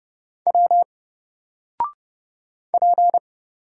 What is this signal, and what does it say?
Morse code, heard as a keyed tone, about 30 wpm, 700 Hz. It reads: W P
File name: Single-Letter-Number-30wpm